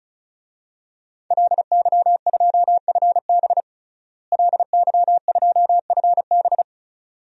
Morse code, heard as a keyed tone, about 36 wpm, 700 Hz. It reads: LY2FB LY2FB
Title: DXCC-100 Call Signs - 5 Characters - Repeated 2x - 35wpm